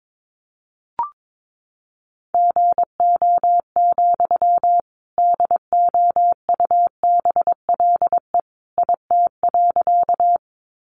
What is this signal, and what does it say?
Morse code, heard as a keyed tone, about 22 wpm, 700 Hz. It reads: GO, DOUBLE IT.
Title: Sentences from Top 1400 Words 22wpm